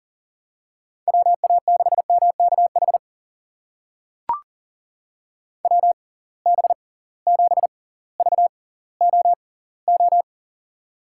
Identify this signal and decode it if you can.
Morse code, heard as a keyed tone, about 40 wpm, spaced as for 20 wpm, 700 Hz. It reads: WA6MKH WB7VOO
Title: USA Call Signs - 6 Characters - 6x Character Spacing - 40wpm